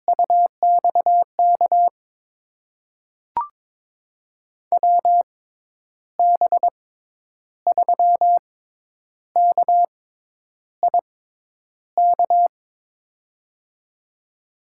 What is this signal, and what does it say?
Morse code, heard as a keyed tone, about 22 wpm, spaced as for 9 wpm, 700 Hz. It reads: UXK WB3KIK
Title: USA Call Signs - 6 Characters - 6x Character Spacing - 22wpm